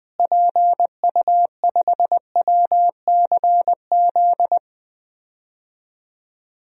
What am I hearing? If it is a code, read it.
Morse code, heard as a keyed tone, about 20 wpm, 700 Hz. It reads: PU5WCZ
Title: DXCC-100 Call Signs - 6 Characters - Repeated 3x - 20wpm